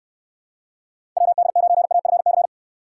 Morse code, heard as a keyed tone, about 64 wpm, 700 Hz. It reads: WD9AFC